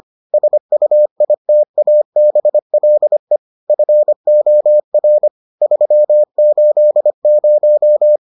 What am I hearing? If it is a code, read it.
Morse code, heard as a keyed tone, about 25 wpm, 600 Hz. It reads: SUITABLE FOR 380